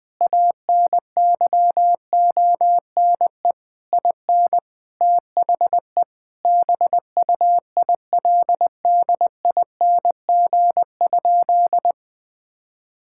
Morse code, heard as a keyed tone, about 21 wpm, 700 Hz. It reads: ANYONE IN THE BUILDING?
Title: Sentences from Top 1000 Words - Review Part 5 - 20wpm